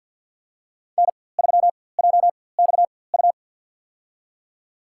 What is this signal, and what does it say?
Morse code, heard as a keyed tone, about 53 wpm, spaced as for 24 wpm, 700 Hz. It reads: N3JXU
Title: USA Call Signs - 5 Characters - 4x Character Spacing - 50wpm